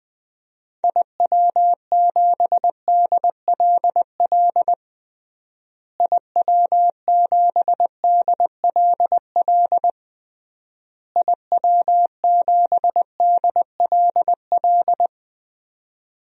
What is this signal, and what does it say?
Morse code, heard as a keyed tone, about 20 wpm, 700 Hz. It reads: IW7DLL IW7DLL IW7DLL
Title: DXCC-100 Call Signs - 6 Characters - Repeated 3x - 20wpm